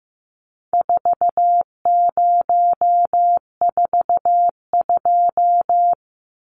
Morse code, heard as a keyed tone, about 15 wpm, 700 Hz. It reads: 4042
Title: Numbers - 4-Digit 15wpm